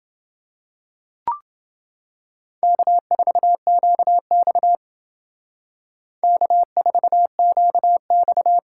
Morse code, heard as a keyed tone, about 28 wpm, 700 Hz. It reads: K4QX K4QX